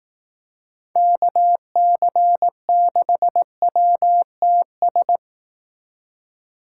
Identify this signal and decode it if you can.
Morse code, heard as a keyed tone, about 18 wpm, 700 Hz. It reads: KC6WTS
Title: Speed Racing - USA 2x3 Call Signs 15wpm